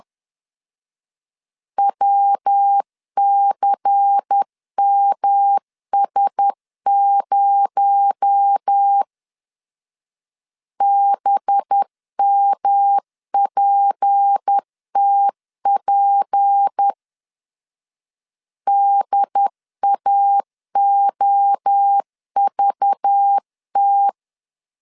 Morse code, heard as a keyed tone, about 10 wpm, 800 Hz. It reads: WCMS0 BMPTP DAOVT
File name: CW / DET / 1 6 1500C S C 10wpm 4252